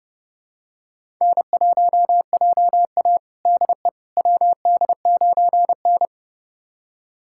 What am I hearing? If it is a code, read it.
Morse code, heard as a keyed tone, about 30 wpm, 700 Hz. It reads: N1JA DE WD9N
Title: USA Callsign DE Callsign - 4 Characters - 30wpm